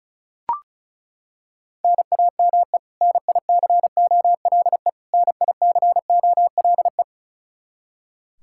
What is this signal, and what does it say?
Morse code, heard as a keyed tone, about 36 wpm, 700 Hz. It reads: NAME NICOLE NICOLE
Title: Practice QSO - Exchange 1 - Beginning - 35wpm